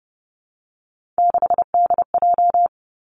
Morse code, heard as a keyed tone, about 30 wpm, 700 Hz. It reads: 6DJ
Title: Three Challenging Letter-Number Combinations 30wpm